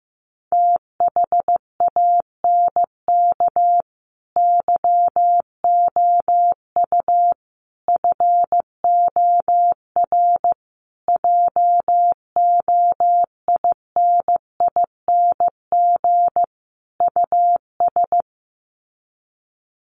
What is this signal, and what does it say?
Morse code, heard as a keyed tone, about 14 wpm, 700 Hz. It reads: THANK YOU FOR JOINING US
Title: Sets of 5 Words 15wpm